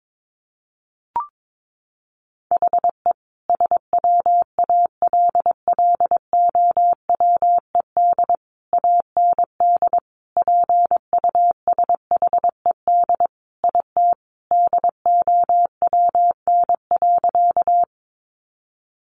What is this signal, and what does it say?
Morse code, heard as a keyed tone, about 22 wpm, 700 Hz. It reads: HE SWALLOWED AND PUSHED IT DOWN.
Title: A Sound Of Thunder 22wpm